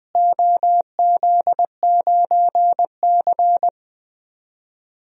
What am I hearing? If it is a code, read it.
Morse code, heard as a keyed tone, about 20 wpm, 700 Hz. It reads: OZ9C
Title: DXCC-100 Call Signs - 4 Characters - Repeated 2x - 20wpm